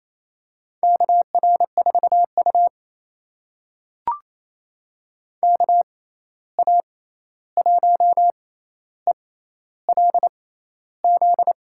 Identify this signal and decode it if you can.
Morse code, heard as a keyed tone, about 28 wpm, spaced as for 13 wpm, 700 Hz. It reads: KR4U KA1ELZ
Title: USA Call Signs - 6x Character Spacing - 28wpm